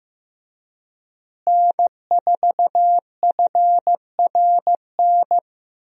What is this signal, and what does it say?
Morse code, heard as a keyed tone, about 15 wpm, 700 Hz. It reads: N4FRN
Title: USA Call Signs - 5 Characters - Repeated 3x - 15wpm